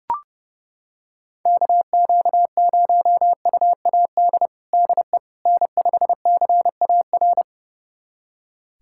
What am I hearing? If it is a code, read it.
Morse code, heard as a keyed tone, about 30 wpm, 700 Hz. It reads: KQ0UAD DE N5CAR
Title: USA Callsign DE Callsign 30wpm